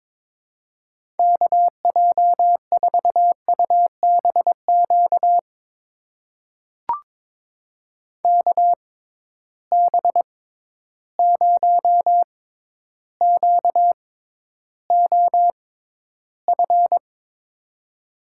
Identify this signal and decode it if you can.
Morse code, heard as a keyed tone, about 22 wpm, spaced as for 11 wpm, 700 Hz. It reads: KJ4UBQ KB0QOF
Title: USA Call Signs - 6 Characters - 6x Character Spacing - 22wpm